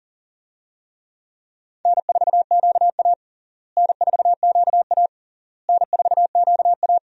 Morse code, heard as a keyed tone, about 40 wpm, 700 Hz. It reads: N4QA N4QA N4QA
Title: USA Call Signs - 4 Characters - Repeated 3x - 40wpm